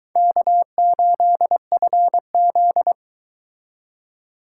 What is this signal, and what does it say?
Morse code, heard as a keyed tone, about 24 wpm, 700 Hz. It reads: K8FZ